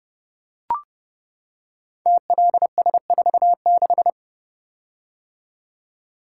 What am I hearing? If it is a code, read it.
Morse code, heard as a keyed tone, about 30 wpm, 700 Hz. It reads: TLS46
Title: Code Groups - 5 Letter:Numbers - 30wpm